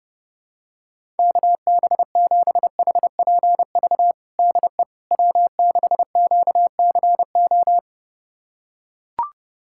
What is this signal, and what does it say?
Morse code, heard as a keyed tone, about 30 wpm, 700 Hz. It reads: KB7HPV DE W6QCO
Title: USA Callsign DE Callsign 30wpm